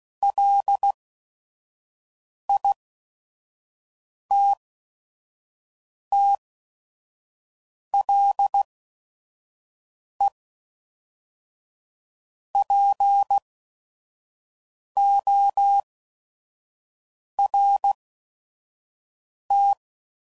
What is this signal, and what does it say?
Morse code, heard as a keyed tone, about 16 wpm, spaced as for 5 wpm, 800 Hz. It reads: LITTLEPORT
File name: CW Code Groups @5WPM / fromSM's / Cities